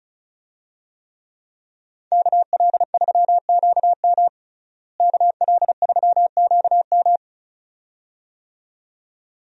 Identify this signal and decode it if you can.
Morse code, heard as a keyed tone, about 34 wpm, 700 Hz. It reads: KL3QM KL3QM
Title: USA Call Signs - 5 Characters - Repeated 2x - 35wpm